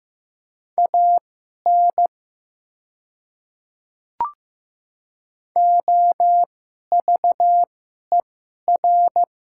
Morse code, heard as a keyed tone, about 15 wpm, spaced as for 11 wpm, 700 Hz. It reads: AN OVER
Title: 2x Character Spacing - Top 100 Words - 15wpm